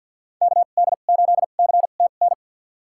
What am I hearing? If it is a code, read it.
Morse code, heard as a keyed tone, about 50 wpm, spaced as for 36 wpm, 700 Hz. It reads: KD8CTN
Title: USA Call Signs - 2x Character Spacing - 50wpm